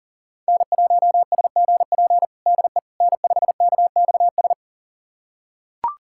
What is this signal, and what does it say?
Morse code, heard as a keyed tone, about 40 wpm, 700 Hz. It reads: N1SGP DE N5KXS
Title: USA Callsign DE Callsign - 5 Characters - 40wpm